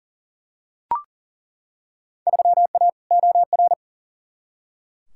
Morse code, heard as a keyed tone, about 40 wpm, 700 Hz. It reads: 3A OR
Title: Field Day Exchanges - 40wpm